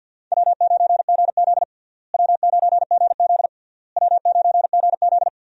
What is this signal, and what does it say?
Morse code, heard as a keyed tone, about 50 wpm, 700 Hz. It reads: W9GZ W9GZ W9GZ
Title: USA Call Signs - 4 Characters - Repeated 3x - 50wpm